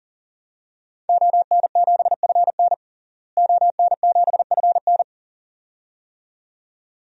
Morse code, heard as a keyed tone, about 40 wpm, 700 Hz. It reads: ON7FN ON7FN
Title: DXCC-100 Call Signs - 5 Characters - Repeated 2x - 40wpm